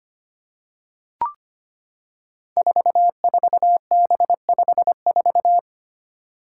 Morse code, heard as a keyed tone, about 25 wpm, 700 Hz. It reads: 44B54
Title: Five Challenging Letter-Number Combinations 25wpm